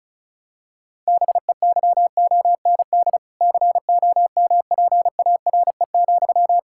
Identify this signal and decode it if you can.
Morse code, heard as a keyed tone, about 36 wpm, 700 Hz. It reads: BEYOND COMPARE,
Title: Sentences from Top 1600 Words 35wpm